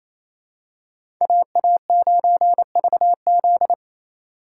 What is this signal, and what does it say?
Morse code, heard as a keyed tone, about 28 wpm, 700 Hz. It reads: AA9VZ
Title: USA Call Signs - 2x Character Spacing - 28wpm